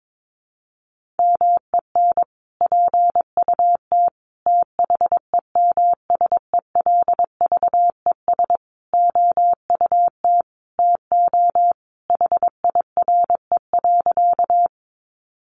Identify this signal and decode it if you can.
Morse code, heard as a keyed tone, about 22 wpm, 700 Hz. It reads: MEN PUT THEMSELVES OUT TO HIRE.
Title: Sentences from Top 1400 Words 22wpm